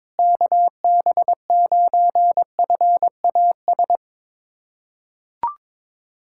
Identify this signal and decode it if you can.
Morse code, heard as a keyed tone, about 22 wpm, 700 Hz. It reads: KB9FAS
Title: USA Callsign DE Callsign 22wpm